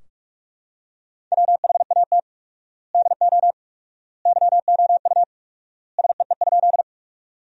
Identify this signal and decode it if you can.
Morse code, heard as a keyed tone, about 43 wpm, 700 Hz. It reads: WHAT DO YOU SEE?
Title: 4x Word-Spacing - Sentences from Top 100 Words - 45wpm